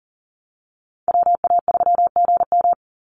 Morse code, heard as a keyed tone, about 40 wpm, 700 Hz. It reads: WA3GM
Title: Call Signs Repeated 2x - 40wpm